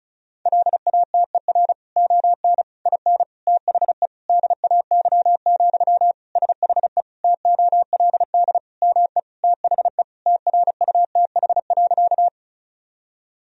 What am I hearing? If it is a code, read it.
Morse code, heard as a keyed tone, about 36 wpm, 700 Hz. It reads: LATER ON IN THE DAY, SHE TOLD ME THE TRUTH.
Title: Sentences from Top 1000 Words - Review Part 4 - 35wpm